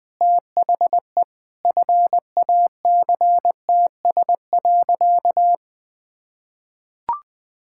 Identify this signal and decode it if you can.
Morse code, heard as a keyed tone, about 20 wpm, 700 Hz. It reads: THE FACTS.
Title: Sentences with 8-Word Phrases - 20wpm